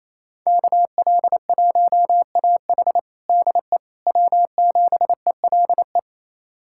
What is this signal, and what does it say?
Morse code, heard as a keyed tone, about 28 wpm, 700 Hz. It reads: KL1AH DE W7ELE
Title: USA Callsign DE Callsign 28wpm